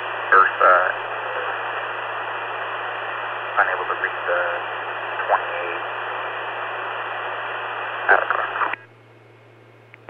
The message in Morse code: none

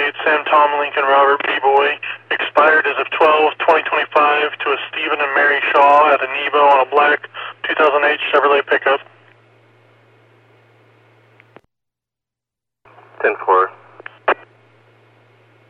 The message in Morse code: none

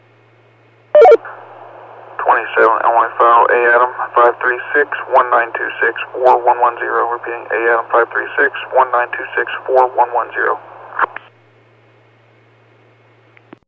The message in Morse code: none

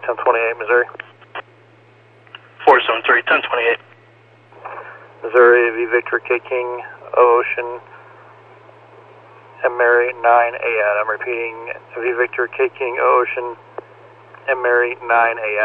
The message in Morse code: none

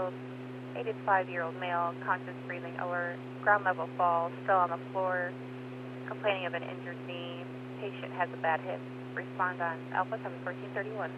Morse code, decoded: none

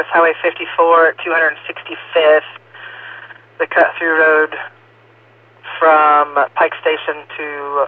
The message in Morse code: none

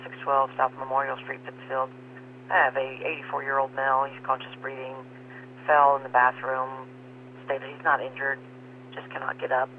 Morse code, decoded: none

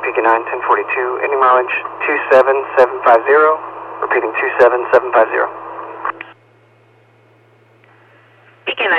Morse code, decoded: none

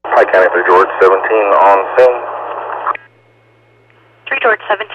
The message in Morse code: none

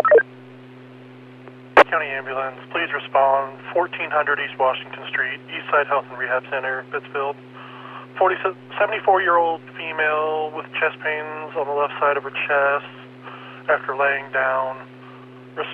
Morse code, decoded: none

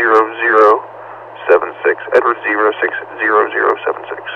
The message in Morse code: none

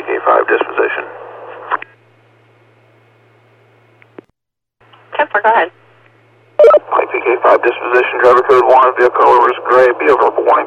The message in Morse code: none

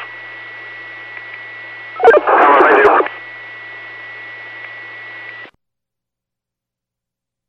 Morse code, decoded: none

